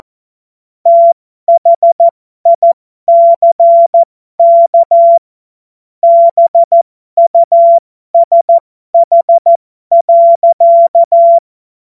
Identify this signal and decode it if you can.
Morse code, heard as a keyed tone, about 14 wpm, spaced as for 12 wpm, 680 Hz. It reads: THICK BUSH.
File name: Morsekurs / Witze